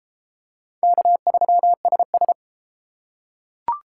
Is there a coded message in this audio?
Yes: Morse code from 0.8 to 2.3 s, heard as a keyed tone, about 30 wpm, 700 Hz. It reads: K3SS